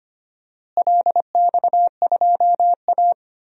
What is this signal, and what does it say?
Morse code, heard as a keyed tone, about 25 wpm, 700 Hz. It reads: LX2A
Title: DXCC-100 Call Signs - 4 Characters - Repeated 2x - 25wpm